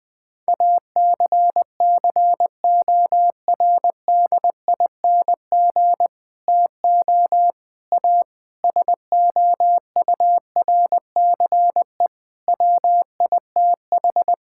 Morse code, heard as a keyed tone, about 21 wpm, 700 Hz. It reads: ACCORDING TO A SOURCE WITH
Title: Sets of 9 Words - 20wpm